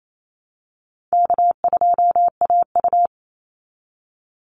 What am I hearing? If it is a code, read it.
Morse code, heard as a keyed tone, about 28 wpm, 700 Hz. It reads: K2AU
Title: Call Signs - 28wpm